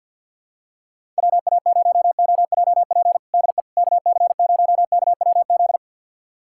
Morse code, heard as a keyed tone, about 52 wpm, 700 Hz. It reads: WA0OJP DE KC0KWZ
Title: USA Callsign DE Callsign - 6 Characters - 50wpm